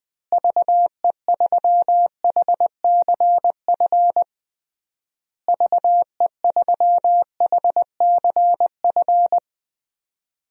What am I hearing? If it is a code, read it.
Morse code, heard as a keyed tone, about 20 wpm, 700 Hz. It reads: VE3HCF VE3HCF
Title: DXCC-100 Call Signs - 6 Characters - Repeated 3x - 20wpm